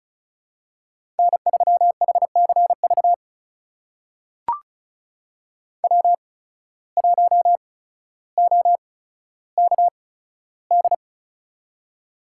Morse code, heard as a keyed tone, about 36 wpm, spaced as for 15 wpm, 700 Hz. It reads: N3HCV W1OKD
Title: USA Call Signs - 5 Characters - 8x Character Spacing - 35wpm